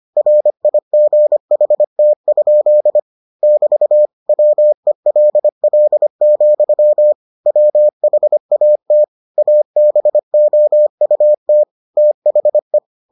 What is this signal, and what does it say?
Morse code, heard as a keyed tone, about 25 wpm, 600 Hz. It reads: RIGHT? <BT> WELL, WHAT ABOUT THE